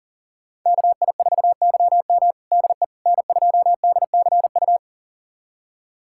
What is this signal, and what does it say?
Morse code, heard as a keyed tone, about 40 wpm, 700 Hz. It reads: KI4YM DE N2DCU
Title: USA Callsign DE Callsign - 5 Characters - 40wpm